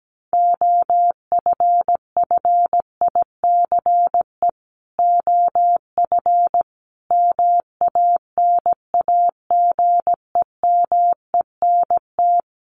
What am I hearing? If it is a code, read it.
Morse code, heard as a keyed tone, about 17 wpm, 700 Hz. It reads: OFFICE OF MANAGEMENT
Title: Sets of 5 Words Encore2 17wpm